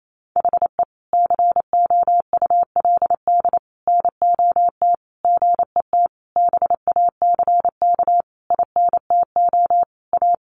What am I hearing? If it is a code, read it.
Morse code, heard as a keyed tone, about 28 wpm, 700 Hz. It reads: HE COULD NOT GET BACK INTO A
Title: Sentences from Top 1100 Words 28wpm